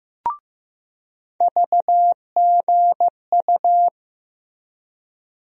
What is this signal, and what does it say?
Morse code, heard as a keyed tone, about 15 wpm, 700 Hz. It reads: VGU
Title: USA Call Sign Postfixes - 3 Letters 15wpm